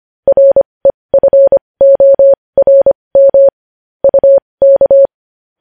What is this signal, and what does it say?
Morse code, heard as a keyed tone, about 25 wpm, spaced as for 20 wpm, 550 Hz. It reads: REFORM UK
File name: podcast / News headlines-1x-25x20